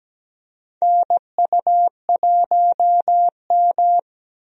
Morse code, heard as a keyed tone, about 17 wpm, 700 Hz. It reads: NU1M
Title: USA Call Signs - 4 Characters - Repeated 3x - 17wpm